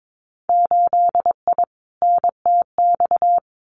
Morse code, heard as a keyed tone, about 22 wpm, 700 Hz. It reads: 8I NTX
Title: Winter Field Day Exchanges Encore 22wpm